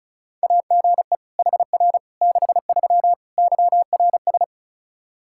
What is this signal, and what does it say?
Morse code, heard as a keyed tone, about 36 wpm, 700 Hz. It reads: AGE HR 63 YRS
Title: Practice QSO - Exchange 3 - More Info - 35wpm